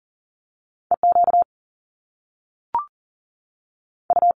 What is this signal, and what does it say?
Morse code, heard as a keyed tone, about 38 wpm, spaced as for 3 wpm, 700 Hz. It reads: E QU
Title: Top Two Letter Combinations in Words 40wpm